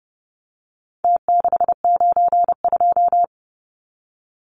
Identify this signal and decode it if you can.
Morse code, heard as a keyed tone, about 30 wpm, 700 Hz. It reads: T692
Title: Numbers - Contest Style - 4-Digit 30wpm